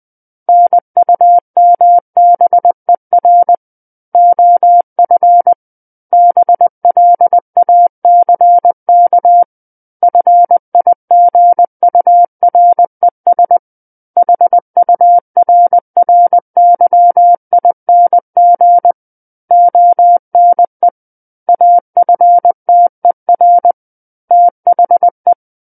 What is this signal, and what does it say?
Morse code, heard as a keyed tone, about 20 wpm, 700 Hz. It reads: NUMBER OF BLACK FIGURES HURRYING ONE AFTER THE
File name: 20 WPM